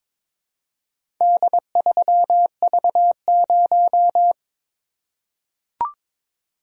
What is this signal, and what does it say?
Morse code, heard as a keyed tone, about 22 wpm, 700 Hz. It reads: D3V0
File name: practice / 22wpm / Four Letter-Number Combinations - 22wpm